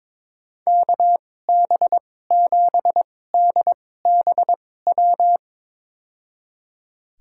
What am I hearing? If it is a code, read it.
Morse code, heard as a keyed tone, about 22 wpm, spaced as for 16 wpm, 700 Hz. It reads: KB7DBW
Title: USA Call Signs - 2x Character Spacing - 22wpm